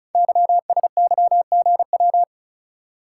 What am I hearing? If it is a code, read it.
Morse code, heard as a keyed tone, about 36 wpm, 700 Hz. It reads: YSYGW
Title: Code Groups - 5 Letters - 35wpm